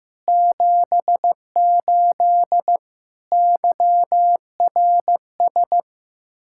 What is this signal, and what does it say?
Morse code, heard as a keyed tone, about 16 wpm, 700 Hz. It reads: 78 YRS
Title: Practice QSO - Exchange 3 - More Info - 15wpm